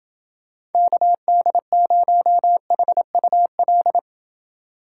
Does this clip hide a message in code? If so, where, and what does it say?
Yes: Morse code from 0.7 to 4.0 s, heard as a keyed tone, about 30 wpm, 700 Hz. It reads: KD0HUL